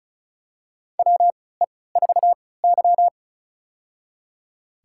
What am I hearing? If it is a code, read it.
Morse code, heard as a keyed tone, about 36 wpm, spaced as for 20 wpm, 700 Hz. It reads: WE4Y
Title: USA Call Signs - 3x Character Spacing - 35wpm